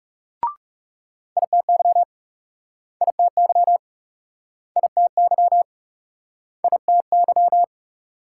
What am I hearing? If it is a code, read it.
Morse code, heard as a keyed tone, about 39 wpm, 700 Hz. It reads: ITY ITY ITY ITY